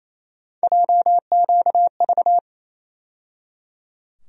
Morse code, heard as a keyed tone, about 28 wpm, 700 Hz. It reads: JQV
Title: USA Call Sign Postfixes - 3 Letters 28wpm